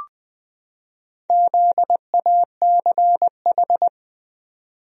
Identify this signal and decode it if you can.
Morse code, heard as a keyed tone, about 20 wpm, 700 Hz. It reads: ZACH
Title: USA Common Names - Max 4 Letters - 20wpm